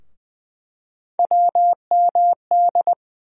Morse code, heard as a keyed tone, about 20 wpm, 700 Hz. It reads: WMD
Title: USA Call Sign Postfixes - 3 Letters 20wpm